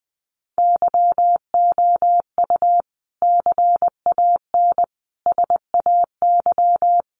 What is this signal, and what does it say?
Morse code, heard as a keyed tone, about 19 wpm, 700 Hz. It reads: YOU CAN SAY
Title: Sentences from Top 200 Words 20wpm